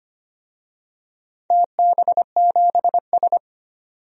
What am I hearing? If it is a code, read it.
Morse code, heard as a keyed tone, about 25 wpm, 700 Hz. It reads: TB7S
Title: DXCC-100 Call Signs - 4 Characters - 25wpm